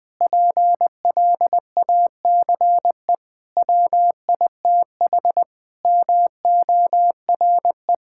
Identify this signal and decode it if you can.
Morse code, heard as a keyed tone, about 20 wpm, 700 Hz. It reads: PLACE WITH MORE
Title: Sentences with 7-Word Phrases - 20wpm